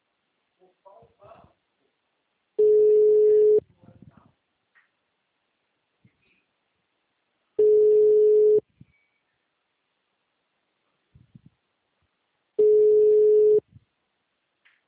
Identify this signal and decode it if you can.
Ringback tone (Europe, 425 Hz), with speech.